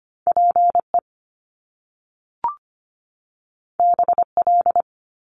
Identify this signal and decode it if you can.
Morse code, heard as a keyed tone, about 25 wpm, 700 Hz. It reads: PE BL
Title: Top Two Letter Combinations in Words 25wpm